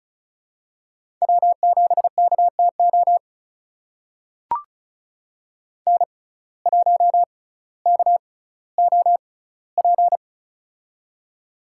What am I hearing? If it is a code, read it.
Morse code, heard as a keyed tone, about 36 wpm, spaced as for 18 wpm, 700 Hz. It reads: W7KTO N1KOP
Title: USA Call Signs - 5 Characters - 6x Character Spacing - 35wpm